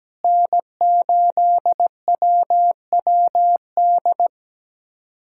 Morse code, heard as a keyed tone, about 17 wpm, 700 Hz. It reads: N8WWD
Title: USA Callsign DE Callsign 17wpm